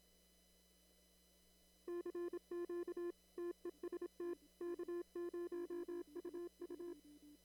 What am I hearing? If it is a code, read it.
Morse code, heard as a keyed tone, about 27 wpm, 340 Hz. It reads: CQ TEST K0UU
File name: CW / kt5j wpx cw-op-k5pi